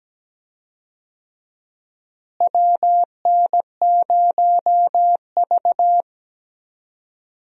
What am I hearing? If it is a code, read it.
Morse code, heard as a keyed tone, about 17 wpm, 700 Hz. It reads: WN0V
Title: USA Call Signs - 4 Characters - Repeated 2x - 17wpm